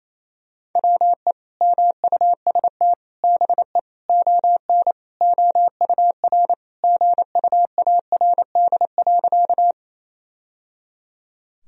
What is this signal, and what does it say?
Morse code, heard as a keyed tone, about 28 wpm, 700 Hz. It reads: WE MUST BE ON OUR GUARD.
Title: Sentences from Top 1000 Words - Review Part 4 - 28wpm